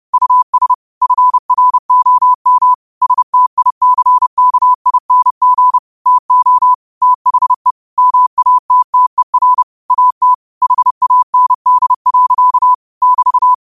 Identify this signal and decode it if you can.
Morse code, heard as a keyed tone, about 30 wpm, 1000 Hz. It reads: US FROM STICKING TO THE MATTER AT HAND. <BT>